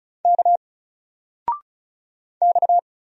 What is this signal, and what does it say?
Morse code, heard as a keyed tone, about 36 wpm, 700 Hz. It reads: K X
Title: Single Letter-Number 35wpm